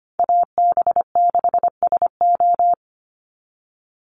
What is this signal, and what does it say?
Morse code, heard as a keyed tone, about 25 wpm, 700 Hz. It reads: AB6SO
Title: NA SOTA Chaser Callsigns 25wpm